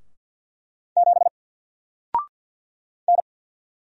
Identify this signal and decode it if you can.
Morse code, heard as a keyed tone, about 48 wpm, 700 Hz. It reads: 7 N
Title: Single Letter-Number Alt 50wpm